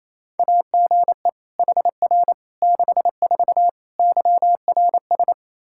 Morse code, heard as a keyed tone, about 28 wpm, 700 Hz. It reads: AGE HR 64 YRS
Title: K1USN SST QSO Practice 28wpm